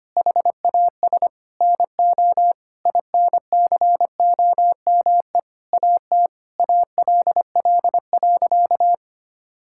Morse code, heard as a keyed tone, about 25 wpm, 700 Hz. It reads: HAS NO INCOME AT ALL.
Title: Sentences from Top 1000 Words - Review Part 5 - 25wpm